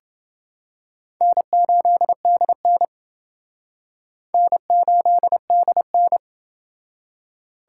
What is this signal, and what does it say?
Morse code, heard as a keyed tone, about 28 wpm, 700 Hz. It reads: N8DN N8DN